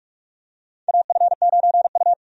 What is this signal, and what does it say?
Morse code, heard as a keyed tone, about 42 wpm, 700 Hz. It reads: AF9U
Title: USA Call Signs - 4 Characters - Repeated 2x - 45wpm